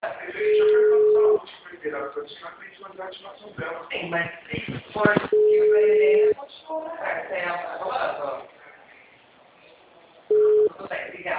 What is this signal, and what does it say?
Ringback tone (Europe, 425 Hz), with speech.